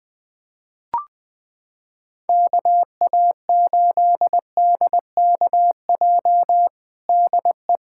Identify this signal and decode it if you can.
Morse code, heard as a keyed tone, about 20 wpm, 700 Hz. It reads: KA8DKJ DE
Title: USA Callsign DE Callsign - 6 Characters - 20wpm